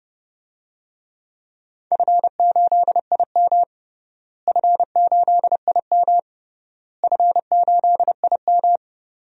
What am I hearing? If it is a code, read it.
Morse code, heard as a keyed tone, about 30 wpm, 700 Hz. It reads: F8IM F8IM F8IM
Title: DXCC-100 Call Signs - 4 Characters - Repeated 3x - 30wpm